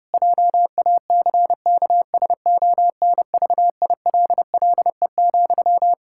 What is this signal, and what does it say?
Morse code, heard as a keyed tone, about 30 wpm, 700 Hz. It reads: JACKSONVILLE,
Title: Practice QSO - Exchange 1 - Beginning - 30wpm